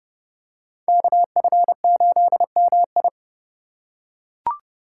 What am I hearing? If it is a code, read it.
Morse code, heard as a keyed tone, about 28 wpm, 700 Hz. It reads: KF8MI